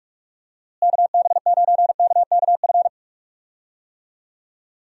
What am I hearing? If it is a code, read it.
Morse code, heard as a keyed tone, about 41 wpm, 700 Hz. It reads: KB9KKF
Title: USA Call Signs - 6 Characters - Repeated 2x - 45wpm